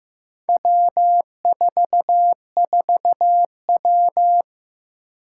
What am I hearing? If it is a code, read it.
Morse code, heard as a keyed tone, about 15 wpm, 700 Hz. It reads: W44W
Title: Four Challenging Letter-Number Combinations 15wpm